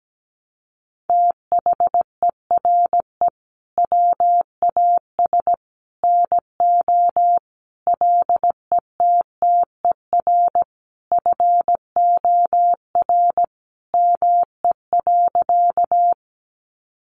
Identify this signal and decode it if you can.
Morse code, heard as a keyed tone, about 17 wpm, 700 Hz. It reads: THERE WAS NO LETTER FOR ME.
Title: Sentences from Top 300 Words 17wpm